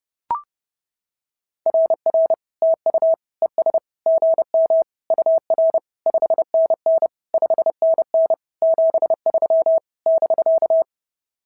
Morse code, heard as a keyed tone, about 26 wpm, 650 Hz. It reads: RR TU ES GM UR 5NN 5NN 73 <BK>
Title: SOTA QSO Practice 30wpm